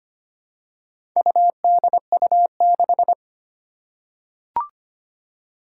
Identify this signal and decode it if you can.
Morse code, heard as a keyed tone, about 25 wpm, 700 Hz. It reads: UDU6
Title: Four Challenging Letter-Number Combinations 25wpm